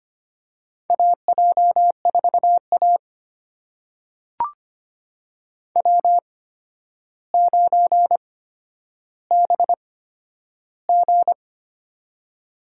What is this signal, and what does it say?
Morse code, heard as a keyed tone, about 25 wpm, spaced as for 11 wpm, 700 Hz. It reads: AJ4A W9BG
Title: USA Call Signs - 4 Characters - 8x Character Spacing - 25wpm